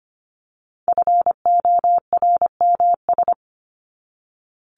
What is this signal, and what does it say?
Morse code, heard as a keyed tone, about 25 wpm, 700 Hz. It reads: FORMS
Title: ES:S Suffix - Top 100 Words with Base Words 25wpm